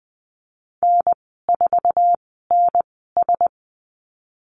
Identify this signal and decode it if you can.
Morse code, heard as a keyed tone, about 20 wpm, spaced as for 15 wpm, 700 Hz. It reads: N4NS
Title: 048 Lesson 9g - Number 4 - Call Signs 20wpm